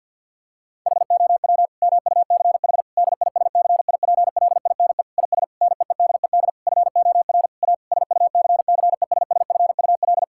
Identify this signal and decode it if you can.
Morse code, heard as a keyed tone, about 50 wpm, 700 Hz. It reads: HOW MUCH DISCIPLINE IS NEEDED FOR A SUCCESSFUL
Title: Sentences from Top 1000 Words - Review Part 6 - 50wpm